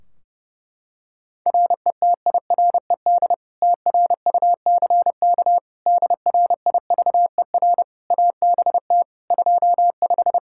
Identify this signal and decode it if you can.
Morse code, heard as a keyed tone, about 30 wpm, 700 Hz. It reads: RETIRED TRUCK DRIVER ABT 25
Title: Practice QSO - Exchange 3 - More Info - 30wpm